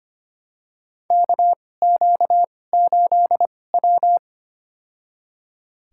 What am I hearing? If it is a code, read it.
Morse code, heard as a keyed tone, about 25 wpm, spaced as for 18 wpm, 700 Hz. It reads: KQ8W
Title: USA Call Signs - 4 Characters - 2x Character Spacing - 25wpm